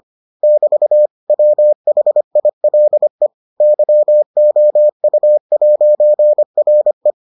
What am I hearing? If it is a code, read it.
Morse code, heard as a keyed tone, about 25 wpm, 600 Hz. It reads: <BT> WHILE YOU'RE